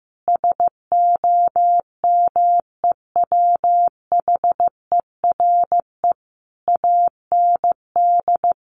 Morse code, heard as a keyed tone, about 15 wpm, 700 Hz. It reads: SOMEWHERE AND
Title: Sentences from Top 1400 Words 15wpm